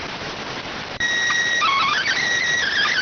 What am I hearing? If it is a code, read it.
SSTV (slow-scan television), PD 290.